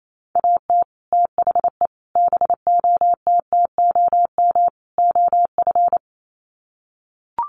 Morse code, heard as a keyed tone, about 28 wpm, 700 Hz. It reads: AT THE BOTTOM OF
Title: Sets of 4 Words 28wpm